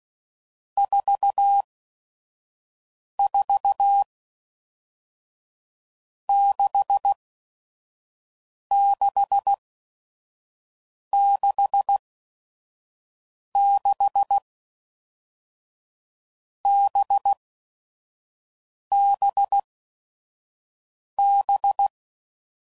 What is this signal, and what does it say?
Morse code, heard as a keyed tone, about 16 wpm, spaced as for 5 wpm, 800 Hz. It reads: 44 6666 BBB
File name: Groups / Basic Letter Groups / VB46